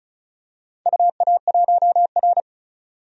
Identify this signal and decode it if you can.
Morse code, heard as a keyed tone, about 36 wpm, 700 Hz. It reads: UA1R
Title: DXCC-100 Call Signs - 4 Characters - 35wpm